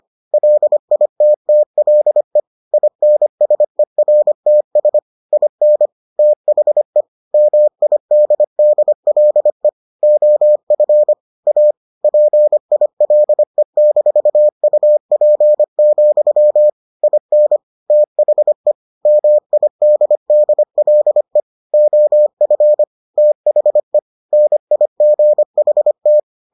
Morse code, heard as a keyed tone, about 25 wpm, 600 Hz. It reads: LITTLE INSERTS IN THE MIDDLE OF A PILE-UP, IN THE MIDDLE OF THE NIGHT